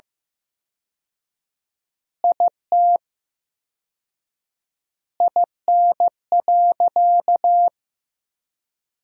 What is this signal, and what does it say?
Morse code, heard as a keyed tone, about 16 wpm, 700 Hz. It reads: IT IN.
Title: 4x Word-Spacing - Sentences from Top 200 Words - 15wpm